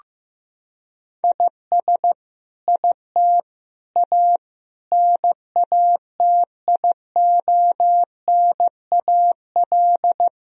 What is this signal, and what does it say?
Morse code, heard as a keyed tone, about 15 wpm, 700 Hz. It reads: IS IT A NATIONAL
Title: Sentences from Top 1000 Words - Review Part 4 - 15wpm